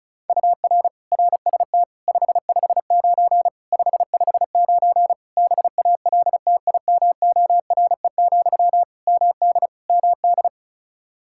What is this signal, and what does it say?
Morse code, heard as a keyed tone, about 34 wpm, 700 Hz. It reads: UR RST 559 559 BALTIMORE, MD MD
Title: Practice QSO - Exchange 1 - Beginning - 35wpm